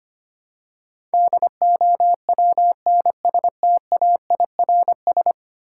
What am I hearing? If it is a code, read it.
Morse code, heard as a keyed tone, about 25 wpm, 700 Hz. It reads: DOWNSTAIRS
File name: 10 Letter Words - First 3 Letters Given - 25wpm